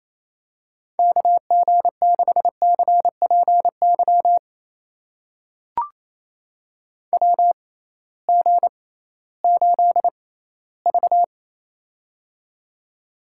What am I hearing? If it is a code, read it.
Morse code, heard as a keyed tone, about 28 wpm, spaced as for 16 wpm, 700 Hz. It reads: KG6CPY WG8V
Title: USA Call Signs - 6x Character Spacing - 28wpm